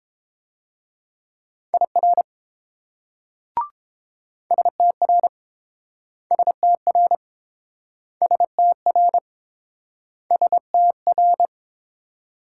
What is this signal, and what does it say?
Morse code, heard as a keyed tone, about 29 wpm, 700 Hz. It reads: IR STR STR STR STR